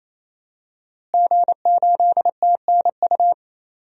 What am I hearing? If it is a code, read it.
Morse code, heard as a keyed tone, about 28 wpm, 700 Hz. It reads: G8TNU
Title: DXCC-100 Call Signs - 5 Characters - Repeated 3x - 28wpm